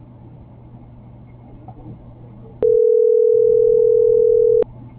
Ringback tone (North America), 440 + 480 Hz.